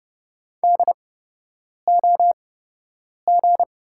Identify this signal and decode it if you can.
Morse code, heard as a keyed tone, about 30 wpm, spaced as for 8 wpm, 700 Hz. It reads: DOG
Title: 8x Character Spacing - Top 500 Words - 30wpm